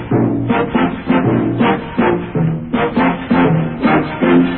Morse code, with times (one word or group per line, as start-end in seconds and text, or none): none